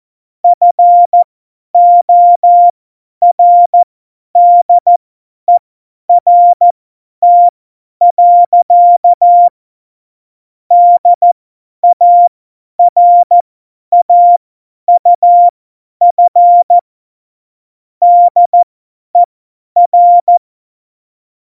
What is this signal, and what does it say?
Morse code, heard as a keyed tone, about 14 wpm, spaced as for 10 wpm, 700 Hz. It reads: FORDERT. DARAUF DER
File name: Morsekurs / Witze / Witz 29. Januar 2019, 50 F70